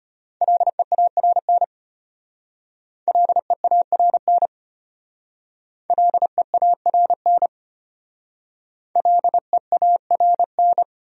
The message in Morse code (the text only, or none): LEARN LEARN LEARN LEARN